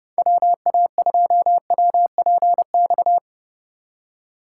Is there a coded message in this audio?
Yes: Morse code from 0.2 to 3.2 s, heard as a keyed tone, about 32 wpm, 700 Hz. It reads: WA2WPX